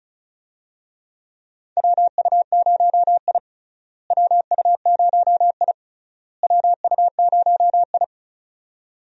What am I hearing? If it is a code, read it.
Morse code, heard as a keyed tone, about 36 wpm, 700 Hz. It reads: WU0I WU0I WU0I
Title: USA Call Signs - 4 Characters - Repeated 3x - 35wpm